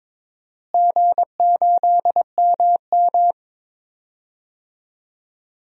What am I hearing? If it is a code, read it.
Morse code, heard as a keyed tone, about 22 wpm, 700 Hz. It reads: G8MM
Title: DXCC-100 Call Signs - 4 Characters - Repeated 2x - 22wpm